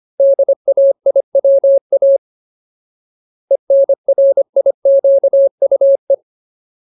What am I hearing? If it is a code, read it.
Morse code, heard as a keyed tone, about 25 wpm, 550 Hz. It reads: DAIWA ENRIQUE
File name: Ultimate Word List / 25WPM Ultimate